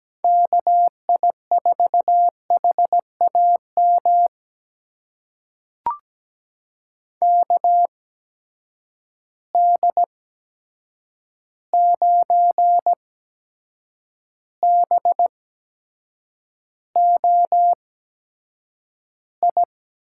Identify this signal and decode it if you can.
Morse code, heard as a keyed tone, about 17 wpm, spaced as for 7 wpm, 700 Hz. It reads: KI4HAM KD9BOI
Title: USA Call Signs - 6 Characters - 8x Character Spacing - 17wpm